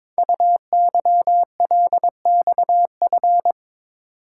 Morse code, heard as a keyed tone, about 22 wpm, 700 Hz. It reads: UYLXF
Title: Code Groups - 5 Letters - 22wpm